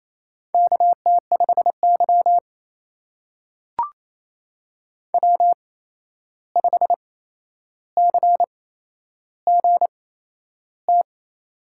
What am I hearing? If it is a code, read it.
Morse code, heard as a keyed tone, about 27 wpm, spaced as for 11 wpm, 700 Hz. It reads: KT5Y W5CGT
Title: USA Call Signs - 8x Character Spacing - 28wpm